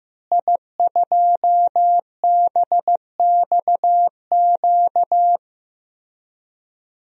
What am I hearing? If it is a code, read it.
Morse code, heard as a keyed tone, about 15 wpm, 700 Hz. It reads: I2BXQ
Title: DXCC-100 Call Signs - 5 Characters - Repeated 3x - 15wpm